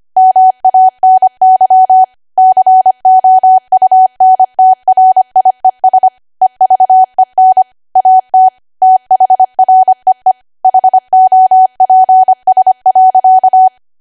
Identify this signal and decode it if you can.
Morse code, heard as a keyed tone, about 25 wpm, 750 Hz. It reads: MANY COUNTRIES EVEN AT THREE HOPS.